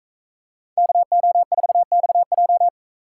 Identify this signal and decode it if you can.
Morse code, heard as a keyed tone, about 44 wpm, 700 Hz. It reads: KO4XJ